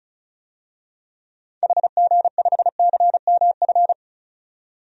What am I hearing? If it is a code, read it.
Morse code, heard as a keyed tone, about 34 wpm, 700 Hz. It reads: HG5CMF
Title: DXCC-100 Call Signs - 6 Characters - 35wpm